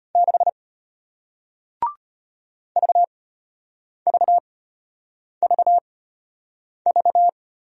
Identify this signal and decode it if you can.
Morse code, heard as a keyed tone, about 33 wpm, spaced as for 8 wpm, 700 Hz. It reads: 6 VVVV